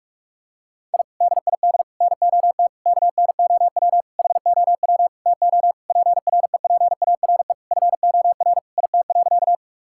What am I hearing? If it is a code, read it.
Morse code, heard as a keyed tone, about 46 wpm, 700 Hz. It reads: I DID NOT KNOW HOW TO PREPARE FOR IT.
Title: Sentences from Top 1000 Words - Review Part 6 - 45wpm